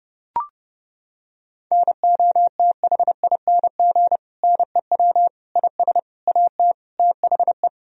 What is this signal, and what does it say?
Morse code, heard as a keyed tone, about 32 wpm, 700 Hz. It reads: NOTHING NEW IS AT THE
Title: Sentences from Top 1000 Words - Review Part 1 - 30wpm